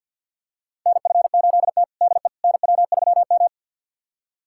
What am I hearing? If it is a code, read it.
Morse code, heard as a keyed tone, about 49 wpm, 700 Hz. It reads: NF8T DE NW3M
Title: USA Callsign DE Callsign - 4 Characters - 50wpm